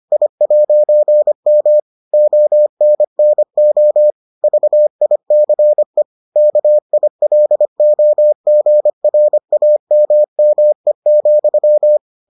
Morse code, heard as a keyed tone, about 25 wpm, 600 Hz. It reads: I'M ONNO VICE KILOGRAMME,